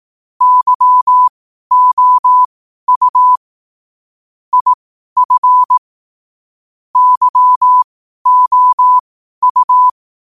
Morse code, heard as a keyed tone, about 18 wpm, spaced as for 12 wpm, 1000 Hz. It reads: YOU IF YOU
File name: qotdfiles / 12wpm